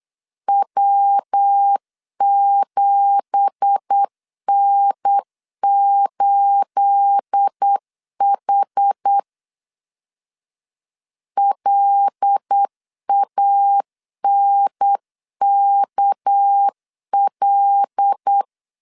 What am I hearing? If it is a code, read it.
Morse code, heard as a keyed tone, about 8 wpm, 800 Hz. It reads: W7N8H LANKL
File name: CW / DET / 1 9 1500C S B 8wpm 5624